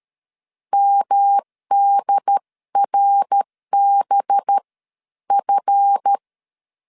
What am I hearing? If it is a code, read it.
Morse code, heard as a keyed tone, about 13 wpm, spaced as for 11 wpm, 800 Hz. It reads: MDRBF
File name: LEZIONI / 7 GF9Z / 7 GF 12wpm 2122